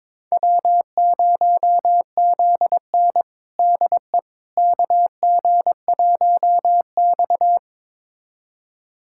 Morse code, heard as a keyed tone, about 22 wpm, 700 Hz. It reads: W0ZN DE KG1X
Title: USA Callsign DE Callsign - 4 Characters - 22wpm